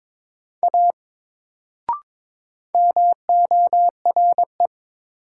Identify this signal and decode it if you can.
Morse code, heard as a keyed tone, about 22 wpm, 700 Hz. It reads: A MORE
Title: Sentences from Top 100 Words - New Word Drill - 22wpm